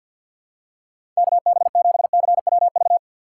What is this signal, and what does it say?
Morse code, heard as a keyed tone, about 52 wpm, 700 Hz. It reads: KB7CWV
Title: USA Call Signs - 6 Characters - 4x Character Spacing - 50wpm